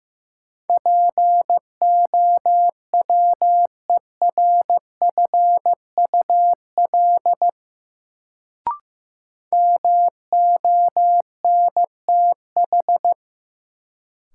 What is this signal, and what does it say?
Morse code, heard as a keyed tone, about 15 wpm, 700 Hz. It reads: POWERFUL MONTH
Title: Sets of 5 Words Encore - New Word Drill - 15wpm